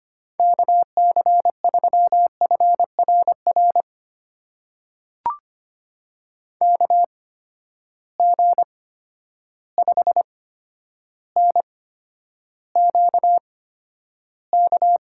Morse code, heard as a keyed tone, about 25 wpm, spaced as for 11 wpm, 700 Hz. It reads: KC3FRR KG5NQK
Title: USA Call Signs - 6 Characters - 8x Character Spacing - 25wpm